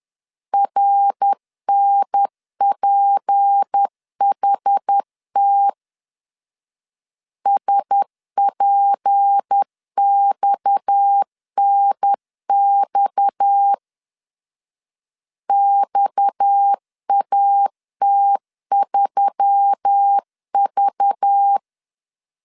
Morse code, heard as a keyed tone, about 10 wpm, 800 Hz. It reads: RNPHT SPXNX XAT3V
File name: LEZIONI / 6 PKX0 / 6 PKX 10wpm 2426